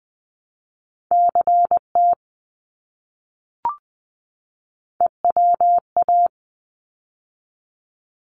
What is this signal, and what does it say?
Morse code, heard as a keyed tone, about 20 wpm, 700 Hz. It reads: CT EWA
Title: ARRL Sections 20wpm